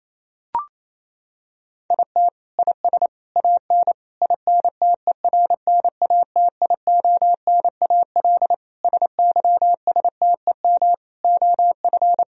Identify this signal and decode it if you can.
Morse code, heard as a keyed tone, about 28 wpm, 700 Hz. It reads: IT IS AN INTERNATIONAL SYSTEM OF
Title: Sentences from Top 1000 Words - Review Part 4 - 28wpm